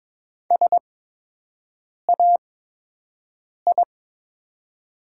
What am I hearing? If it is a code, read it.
Morse code, heard as a keyed tone, about 22 wpm, spaced as for 6 wpm, 700 Hz. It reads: SAI
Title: 8x Character Spacing - Top 300 Words - 22wpm